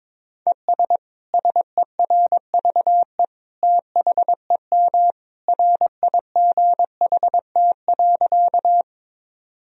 Morse code, heard as a keyed tone, about 22 wpm, 700 Hz. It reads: ES SERVE THEM RIGHT.
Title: Sentences from Top 400 Words and Abbreviations 22wpm